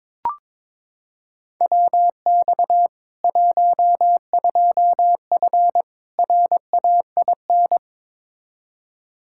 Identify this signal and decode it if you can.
Morse code, heard as a keyed tone, about 21 wpm, 700 Hz. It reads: WX 12F RAIN
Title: K1USN SST QSO Practice 22wpm